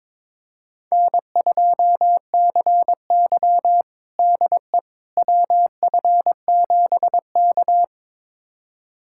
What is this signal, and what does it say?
Morse code, heard as a keyed tone, about 22 wpm, 700 Hz. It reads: N2CY DE WF7K
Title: USA Callsign DE Callsign - 4 Characters - 22wpm